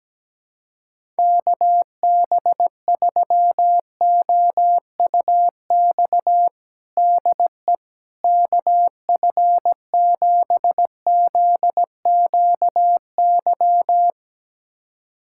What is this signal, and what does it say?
Morse code, heard as a keyed tone, about 17 wpm, 700 Hz. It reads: KB3OUX DE KF7ZQY
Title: USA Callsign DE Callsign - 6 Characters - 17wpm